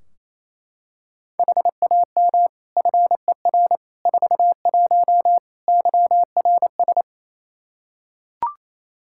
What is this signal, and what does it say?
Morse code, heard as a keyed tone, about 27 wpm, 700 Hz. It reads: HAM FER 41 YRS
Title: Practice QSO - Exchange 3 - More Info - 28wpm